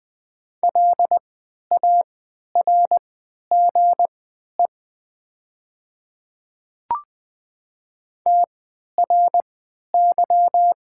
Morse code, heard as a keyed tone, about 20 wpm, spaced as for 11 wpm, 700 Hz. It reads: LARGE TRY
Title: 3x Character Spacing - Top 200 Words - 20wpm